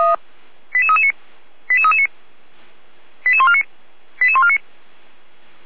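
Five-tone selective call (ZVEI), heard as quick runs of five tones, tones of about 70 ms. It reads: 88339 88339 88149 88149
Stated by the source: ZVEI five-tone call 88199 received by SDR on channel 455, recorded by fire brigade Feuerwehr Orlamünde.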